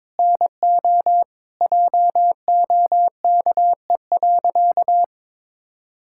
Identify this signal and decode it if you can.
Morse code, heard as a keyed tone, about 22 wpm, 700 Hz. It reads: NO JOKE.
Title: Pangram Challenge v2 - 22wpm